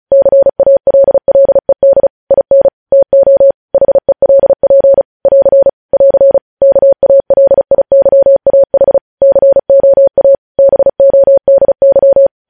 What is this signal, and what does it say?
Morse code, heard as a keyed tone, about 36 wpm, 550 Hz. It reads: CALLED IN TO HELP <AR> <AR> KALIYAH COA BODY
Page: news headlines in morse code at 35 wpm